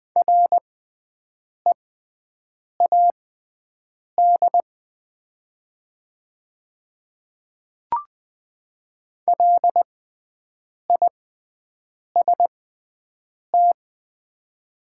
Morse code, heard as a keyed tone, about 20 wpm, spaced as for 7 wpm, 700 Hz. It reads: READ LIST
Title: 6x Character Spacing - Top 500 Words - 20wpm